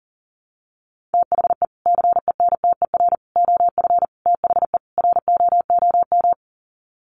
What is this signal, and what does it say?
Morse code, heard as a keyed tone, about 40 wpm, 700 Hz. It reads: THE CENTER OF THE ROOM
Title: Sets of 5 Words Encore 40wpm